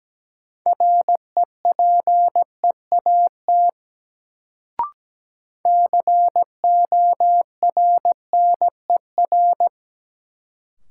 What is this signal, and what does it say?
Morse code, heard as a keyed tone, about 17 wpm, 700 Hz. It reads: REPEAT CORNER
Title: Sentences from Top 1600 Words - New Word Drill Alt - 17wpm